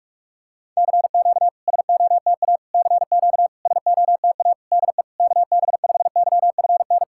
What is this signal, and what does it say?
Morse code, heard as a keyed tone, about 44 wpm, 690 Hz. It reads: CQ SOTA CQ SOTA DE KB5YFN
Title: SOTA QSO Practice 45wpm